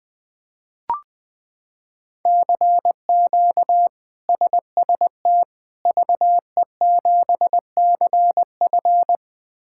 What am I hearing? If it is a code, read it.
Morse code, heard as a keyed tone, about 20 wpm, 700 Hz. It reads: CQ SST VE7CF
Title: K1USN SST QSO Practice - 20wpm